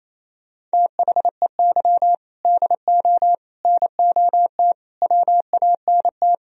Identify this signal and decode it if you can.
Morse code, heard as a keyed tone, about 36 wpm, 700 Hz. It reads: THEY DO NOT WANT